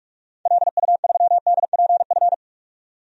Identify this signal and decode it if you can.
Morse code, heard as a keyed tone, about 46 wpm, 700 Hz. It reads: LU3DPF
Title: DXCC-100 Call Signs - 6 Characters - 45wpm